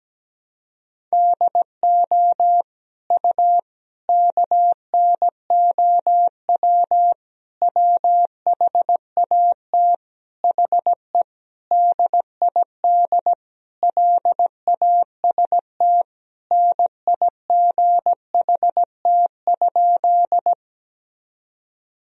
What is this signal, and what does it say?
Morse code, heard as a keyed tone, about 17 wpm, 700 Hz. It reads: DO U KNOW WHAT HE DID LAST NIGHT?
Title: Sentences from Top 300 Words and Abbreviations 17wpm